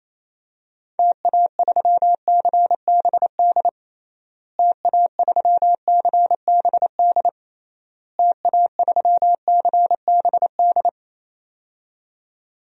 Morse code, heard as a keyed tone, about 28 wpm, 700 Hz. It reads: TA3CBD TA3CBD TA3CBD
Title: DXCC-100 Call Signs - 6 Characters - Repeated 3x - 28wpm